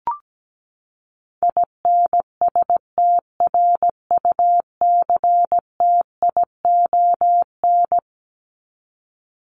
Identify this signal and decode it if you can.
Morse code, heard as a keyed tone, about 17 wpm, 700 Hz. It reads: INSTRUCTION
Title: Top 1751-2000 Words 17wpm